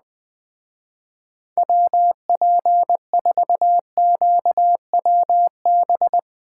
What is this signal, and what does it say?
Morse code, heard as a keyed tone, about 20 wpm, 700 Hz. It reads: WP4QWB
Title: Speed Racing - USA 2x3 Call Signs 17wpm